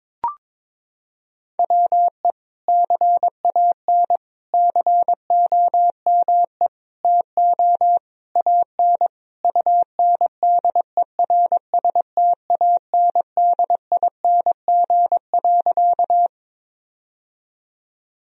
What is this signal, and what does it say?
Morse code, heard as a keyed tone, about 22 wpm, 700 Hz. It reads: WE CAN COME TO AN UNDERSTANDING.
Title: Sentences from Top 1000 Words - Review Part 3 - 22wpm